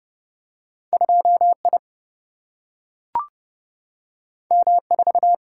Morse code, heard as a keyed tone, about 30 wpm, 700 Hz. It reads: 2I M4
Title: Single-Letter Single-Number 30wpm